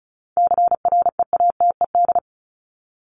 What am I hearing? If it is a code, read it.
Morse code, heard as a keyed tone, about 34 wpm, 700 Hz. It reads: CREATED
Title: ED Suffix - Top 100 Words with Base Words 35wpm